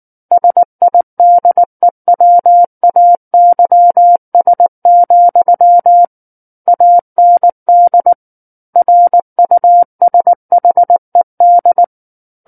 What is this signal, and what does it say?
Morse code, heard as a keyed tone, about 19 wpm, 700 Hz. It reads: SIDEWAYS, AND RUSHED
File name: Book One / 10-Chapter 10 - 19 WPM